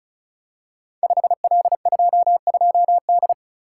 Morse code, heard as a keyed tone, about 36 wpm, 700 Hz. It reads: 5L22D
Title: Code Groups - 5 Letter:Numbers - 35wpm